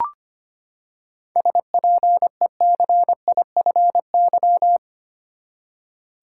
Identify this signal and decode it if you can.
Morse code, heard as a keyed tone, about 25 wpm, 700 Hz. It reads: SPECIFY
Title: Seven Letter Words - 25wpm